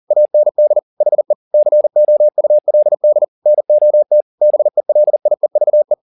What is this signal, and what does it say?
Morse code, heard as a keyed tone, about 40 wpm, 600 Hz. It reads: AND HE COULD NOT BELIEVE
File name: the old man and the sea 2 40wpm